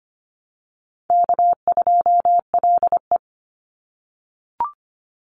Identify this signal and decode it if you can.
Morse code, heard as a keyed tone, about 25 wpm, 700 Hz. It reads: K2LE
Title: Easy Call Signs - 25wpm